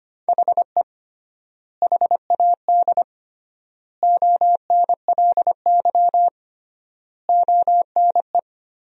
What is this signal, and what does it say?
Morse code, heard as a keyed tone, about 25 wpm, 700 Hz. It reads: HE HAD ONLY ONE
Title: 3x Word-Spacing - Sentences from Top 300 Words - 25wpm